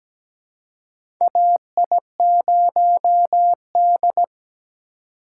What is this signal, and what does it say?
Morse code, heard as a keyed tone, about 17 wpm, 700 Hz. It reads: AI0D
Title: USA Call Signs - 4 Characters - 4x Character Spacing - 17wpm